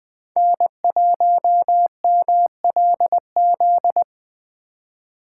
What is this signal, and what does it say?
Morse code, heard as a keyed tone, about 20 wpm, 700 Hz. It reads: N1MLZ